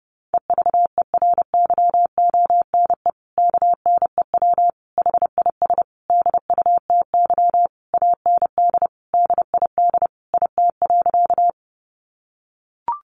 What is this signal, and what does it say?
Morse code, heard as a keyed tone, about 30 wpm, 700 Hz. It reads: EVERYONE KNEW HIS DUTY AND DID IT.
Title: Sentences from Top 1500 Words 30wpm